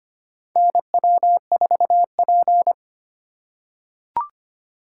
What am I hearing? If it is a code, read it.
Morse code, heard as a keyed tone, about 25 wpm, 700 Hz. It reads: NW4P
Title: USA Call Signs - 4 Characters - 8x Character Spacing - 25wpm